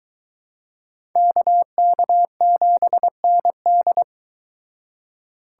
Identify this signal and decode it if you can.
Morse code, heard as a keyed tone, about 22 wpm, 700 Hz. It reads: KK7ND